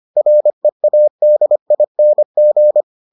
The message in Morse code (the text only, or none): READING